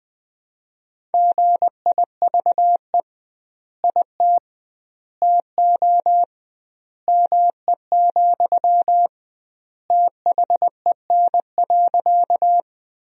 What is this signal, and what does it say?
Morse code, heard as a keyed tone, about 20 wpm, 700 Hz. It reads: GIVE IT TO ME, THEN.
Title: 2x Word-Spacing - Sentences from Top 200 Words - 20wpm